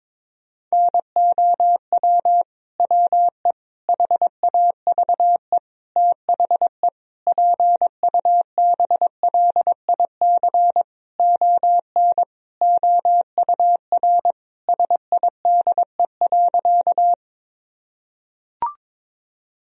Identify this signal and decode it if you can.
Morse code, heard as a keyed tone, about 21 wpm, 700 Hz. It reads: NOW WE HAVE THE PUBLIC ON OUR SIDE.
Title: Sentences from Top 1000 Words - Review Part 3 - 22wpm